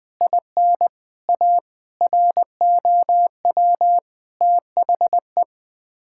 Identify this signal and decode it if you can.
Morse code, heard as a keyed tone, about 20 wpm, 700 Hz. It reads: IN A ROW THE
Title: Sets of 8 Words - 20wpm